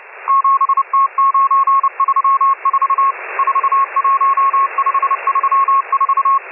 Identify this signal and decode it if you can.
Morse code, heard as a keyed tone, about 30 wpm, 1090 Hz, with radio noise. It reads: 7T934 41534